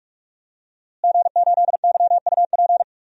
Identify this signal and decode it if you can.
Morse code, heard as a keyed tone, about 41 wpm, 700 Hz. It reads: G8YUP
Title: DXCC-100 Call Signs - 5 Characters - Repeated 2x - 45wpm